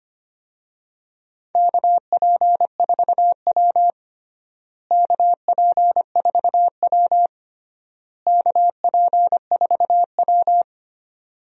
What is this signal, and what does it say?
Morse code, heard as a keyed tone, about 25 wpm, 700 Hz. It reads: KP4W KP4W KP4W
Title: DXCC-100 Call Signs - 4 Characters - Repeated 3x - 25wpm